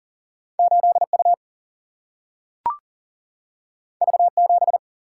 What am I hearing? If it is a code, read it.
Morse code, heard as a keyed tone, about 40 wpm, 700 Hz. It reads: 8 U V 7
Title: Single-Letter Single-Number 40wpm